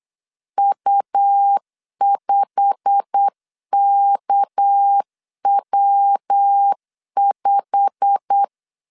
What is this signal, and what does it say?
Morse code, heard as a keyed tone, about 8 wpm, 800 Hz. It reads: U5KW5
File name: LEZIONI / 8 YJ45 / 8 YJ45 8wpm 3214